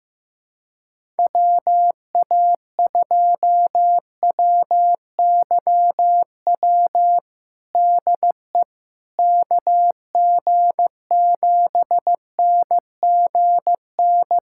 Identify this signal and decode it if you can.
Morse code, heard as a keyed tone, about 15 wpm, 700 Hz. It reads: WA2WYW DE KG7NGN
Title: USA Callsign DE Callsign - 6 Characters - 15wpm